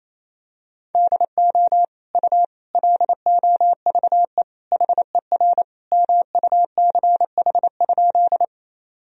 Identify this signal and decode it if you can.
Morse code, heard as a keyed tone, about 28 wpm, 700 Hz. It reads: DO U LOVE HER MUCH?
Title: Sentences from Top 400 Words and Abbreviations 28wpm